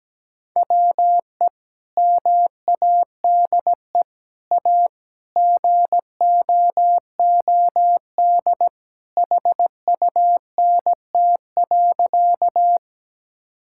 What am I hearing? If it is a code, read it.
Morse code, heard as a keyed tone, about 17 wpm, 700 Hz. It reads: WE MADE A GOOD HUNT.
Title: Sentences from Top 1600 Words 17wpm